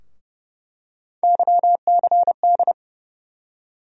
Morse code, heard as a keyed tone, about 30 wpm, 700 Hz. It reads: YCD
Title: USA Call Sign Postfixes - 3 Letters 30wpm